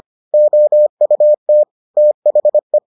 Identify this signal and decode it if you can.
Morse code, heard as a keyed tone, about 25 wpm, 600 Hz. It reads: OUT THE